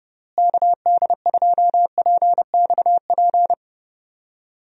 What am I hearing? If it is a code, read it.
Morse code, heard as a keyed tone, about 31 wpm, 700 Hz. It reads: KD2PXP
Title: USA Call Signs - 6 Characters - Repeated 3x - 30wpm